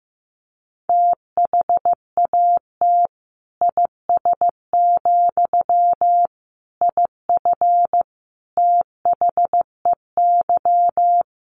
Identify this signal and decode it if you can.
Morse code, heard as a keyed tone, about 14 wpm, 700 Hz. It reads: THAT IS, IF THEY
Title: Sentences from Top 1100 Words 15wpm